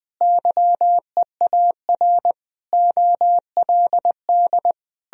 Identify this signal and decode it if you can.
Morse code, heard as a keyed tone, about 21 wpm, 700 Hz. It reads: YEAR OLD
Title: Sentences from Top 1000 Words - Review Part 2 - 20wpm